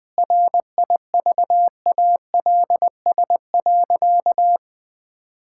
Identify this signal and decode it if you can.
Morse code, heard as a keyed tone, about 20 wpm, 700 Hz. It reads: RIVALS.
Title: Sentences with 7-Word Phrases - 20wpm